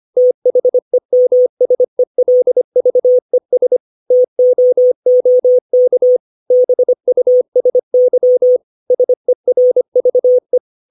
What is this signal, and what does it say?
Morse code, heard as a keyed tone, about 25 wpm, 500 Hz. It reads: THEMSELVES TOOK BUSY SERVE
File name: 632 words / 632 Words 25WPM